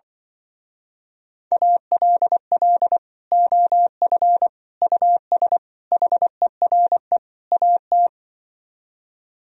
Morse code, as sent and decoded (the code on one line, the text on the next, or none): .- .-.. .-.. / --- ..-. / ..- ... / .... .-. / .- -
ALL OF US HERE AT